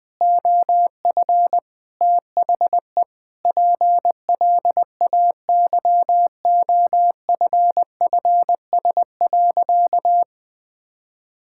Morse code, heard as a keyed tone, about 20 wpm, 700 Hz. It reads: OF THE PLAYOFFS.
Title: Sentences with 7-Word Phrases - 20wpm